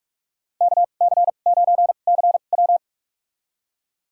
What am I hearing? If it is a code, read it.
Morse code, heard as a keyed tone, about 44 wpm, spaced as for 32 wpm, 700 Hz. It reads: KC9CW
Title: USA Call Signs - 5 Characters - 2x Character Spacing - 45wpm